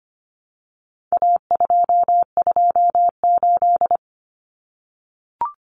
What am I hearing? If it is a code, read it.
Morse code, heard as a keyed tone, about 25 wpm, 700 Hz. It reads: A228
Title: Common Cut Numbers - 4-Digit 25wpm